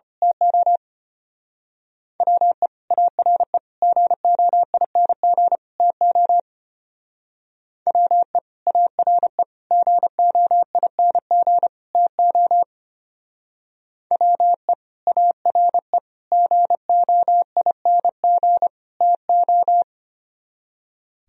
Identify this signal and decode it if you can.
Morse code, heard as a keyed tone, about 29 wpm, 700 Hz. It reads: TO WE ARE GOING TO WE ARE GOING TO WE ARE GOING TO
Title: Speed-Racing - Sets of 4 Words 25wpm